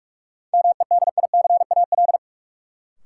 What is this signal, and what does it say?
Morse code, heard as a keyed tone, about 46 wpm, 700 Hz. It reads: MEDICAL
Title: Sentences from Top 800 Words - New Word Drill - 45wpm